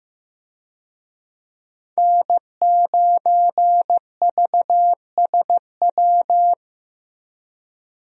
Morse code, heard as a keyed tone, about 15 wpm, 700 Hz. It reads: N9VSW
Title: USA Call Signs - 5 Characters - Repeated 2x - 15wpm